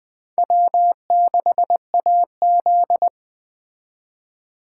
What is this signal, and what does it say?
Morse code, heard as a keyed tone, about 19 wpm, 700 Hz. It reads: W6AZ